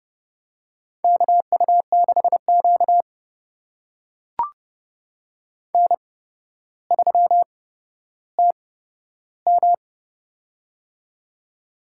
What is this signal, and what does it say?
Morse code, heard as a keyed tone, about 30 wpm, spaced as for 13 wpm, 700 Hz. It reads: KU6Q N3TM
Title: USA Call Signs - 4 Characters - 8x Character Spacing - 30wpm